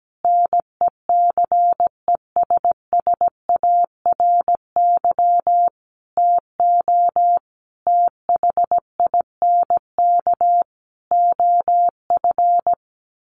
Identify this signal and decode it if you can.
Morse code, heard as a keyed tone, about 17 wpm, 700 Hz. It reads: NECESSARY TO THINK OF
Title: Sentences from Top 1300 Words 17wpm